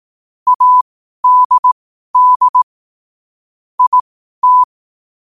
Morse code, heard as a keyed tone, about 18 wpm, spaced as for 12 wpm, 1000 Hz. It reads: ADD IT
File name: 12wpm